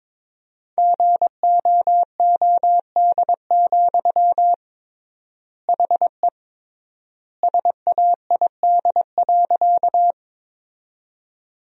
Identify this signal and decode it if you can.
Morse code, heard as a keyed tone, about 22 wpm, 700 Hz. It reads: GOOD, HE SAID.
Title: 3x Word-Spacing - Sentences from Top 200 Words - 22wpm